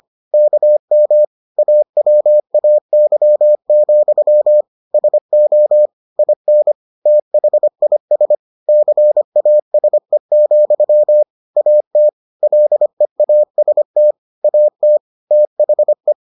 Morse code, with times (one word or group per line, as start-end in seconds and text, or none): 0.3-1.2 KM
1.6-4.6 AWAY,
4.9-5.9 SO
6.2-6.7 IN
7.1-8.3 THIS
8.7-11.2 CASE,
11.6-12.1 AT
12.4-14.1 LEAST
14.4-15.0 AT
15.3-16.1 THE